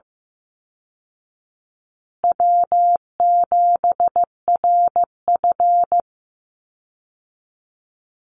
Morse code, heard as a keyed tone, about 15 wpm, 700 Hz. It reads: W7RF
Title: Easy Call Signs Repeated 3x - 15wpm